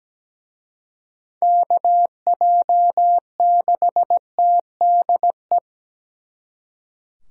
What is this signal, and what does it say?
Morse code, heard as a keyed tone, about 20 wpm, 700 Hz. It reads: KJ6TDE
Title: Speed Racing - USA 2x3 Call Signs 17wpm